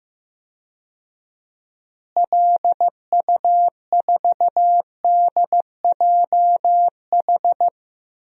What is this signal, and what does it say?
Morse code, heard as a keyed tone, about 15 wpm, 700 Hz. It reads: LU4DJH
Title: DXCC-100 Call Signs - 6 Characters - Repeated 2x - 15wpm